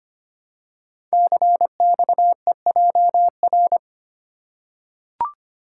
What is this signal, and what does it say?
Morse code, heard as a keyed tone, about 25 wpm, 700 Hz. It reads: CXEJR
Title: Code Groups - 5 Letter:Numbers - 25wpm